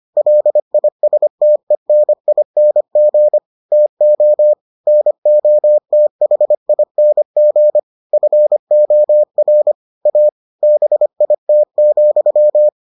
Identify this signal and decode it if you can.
Morse code, heard as a keyed tone, about 25 wpm, 600 Hz. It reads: LISTENING TO NOTHING FOR A BIT,